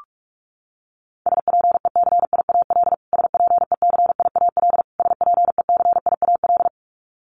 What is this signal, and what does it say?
Morse code, heard as a keyed tone, about 44 wpm, 700 Hz. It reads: SPECIAL SPECIAL SPECIAL
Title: Top 500 Words Repeated 45wpm